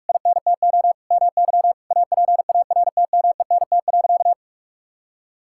Morse code, heard as a keyed tone, about 45 wpm, 700 Hz. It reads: INTO MY APARTMENT.
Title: Sentences from Top 1000 Words - Review Part 6 - 45wpm